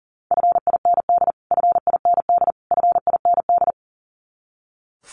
Morse code, heard as a keyed tone, about 40 wpm, 700 Hz. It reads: FIND FIND FIND
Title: Top 200 Words Repeated 40wpm